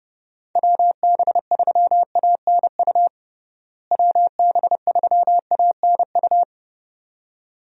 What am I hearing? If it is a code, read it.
Morse code, heard as a keyed tone, about 30 wpm, 700 Hz. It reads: WB3ANU WB3ANU
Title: USA Call Signs - 6 Characters - Repeated 2x - 30wpm